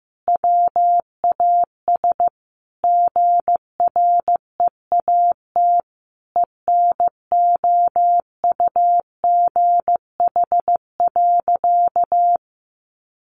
Morse code, heard as a keyed tone, about 15 wpm, 700 Hz. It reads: WAS GREAT ENOUGH.
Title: Sentences from Top 1500 Words 15wpm